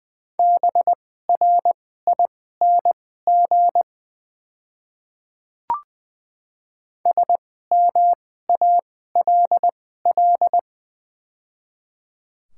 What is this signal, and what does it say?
Morse code, heard as a keyed tone, about 21 wpm, spaced as for 15 wpm, 700 Hz. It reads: BRING SMALL
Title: 2x Character Spacing - Top 500 Words - 20wpm